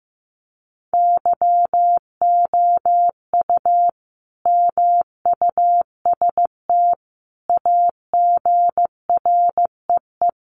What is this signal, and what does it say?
Morse code, heard as a keyed tone, about 15 wpm, 700 Hz. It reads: YOU MUST AGREE
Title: Sentences from Top 1100 Words 15wpm